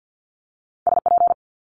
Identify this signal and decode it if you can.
Morse code, heard as a keyed tone, about 50 wpm, 700 Hz. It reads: SP
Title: Multi-letter Phonemes Quick Fire 50wpm